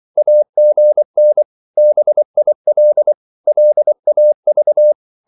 Morse code, heard as a keyed tone, about 24 wpm, 600 Hz. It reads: AGN BIL LAV
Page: Korte danske ord 24wpm